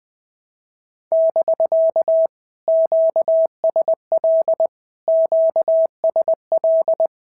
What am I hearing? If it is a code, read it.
Morse code, heard as a keyed tone, about 20 wpm, 650 Hz. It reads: <BK> QSL QSL
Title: POTA QSO Practice - 20wpm